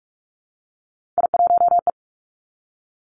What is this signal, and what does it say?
Morse code, heard as a keyed tone, about 46 wpm, 700 Hz. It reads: I1E
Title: Three Challenging Letter-Number Combinations 45wpm